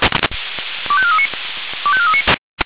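Five-tone selective call (ZVEI), heard as quick runs of five tones, tones of about 70 ms. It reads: 25439 25439
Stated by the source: ZVEI five-tone selective call received on an SDR.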